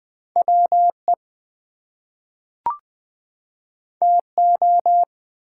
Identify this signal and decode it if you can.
Morse code, heard as a keyed tone, about 20 wpm, 700 Hz. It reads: WE TO
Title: Two Letter Words - 20wpm